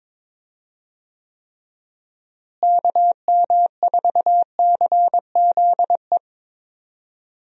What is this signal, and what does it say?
Morse code, heard as a keyed tone, about 22 wpm, 700 Hz. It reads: KM4CZE
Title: USA Call Signs - 6 Characters - Repeated 3x - 22wpm